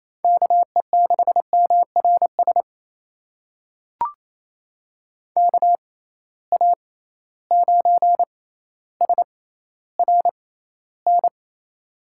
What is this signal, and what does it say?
Morse code, heard as a keyed tone, about 28 wpm, spaced as for 14 wpm, 700 Hz. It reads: KE6MRS KA9SRN
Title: USA Call Signs - 6 Characters - 6x Character Spacing - 28wpm